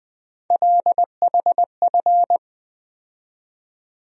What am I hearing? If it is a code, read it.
Morse code, heard as a keyed tone, about 21 wpm, 700 Hz. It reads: LHF
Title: USA Call Sign Postfixes - 3 Letters 20wpm